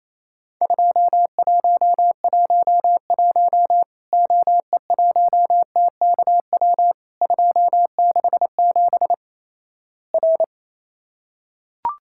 Morse code, heard as a keyed tone, about 28 wpm, 700 Hz. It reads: 2111 OE1TKW 267 R
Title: Worked All Europe DX Contest - QTC Exchange - 28wpm